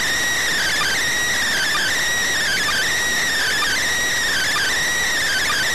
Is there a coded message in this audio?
Yes: Emergency Alert System (SAME) tones from 0.6 to 4.7 s, heard as harsh buzzing bursts.